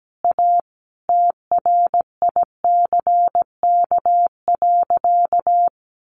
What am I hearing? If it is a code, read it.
Morse code, heard as a keyed tone, about 17 wpm, 700 Hz. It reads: A TRICK.
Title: Sentences from Top 1500 Words 17wpm